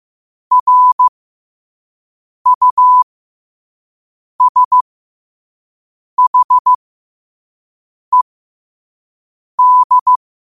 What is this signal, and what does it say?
Morse code, heard as a keyed tone, about 15 wpm, spaced as for 5 wpm, 1000 Hz. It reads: RUSHED
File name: qotdfiles / 5wpm